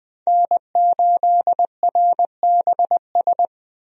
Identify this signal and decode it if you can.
Morse code, heard as a keyed tone, about 20 wpm, 700 Hz. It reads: N8RBS
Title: USA Call Signs - 5 Characters - 20wpm